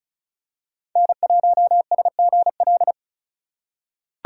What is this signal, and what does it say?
Morse code, heard as a keyed tone, about 36 wpm, 700 Hz. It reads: N1SGL
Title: USA Call Signs - 2x Character Spacing - 35wpm